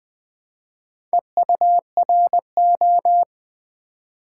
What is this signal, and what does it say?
Morse code, heard as a keyed tone, about 20 wpm, 700 Hz. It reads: EURO
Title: Build A Word - 1 Letter At A Time - 4-Letter Words - 20wpm